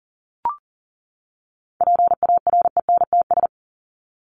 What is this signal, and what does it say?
Morse code, heard as a keyed tone, about 40 wpm, 700 Hz. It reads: PARENTS
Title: ES:S Suffix - Top 100 Words with Base Words 40wpm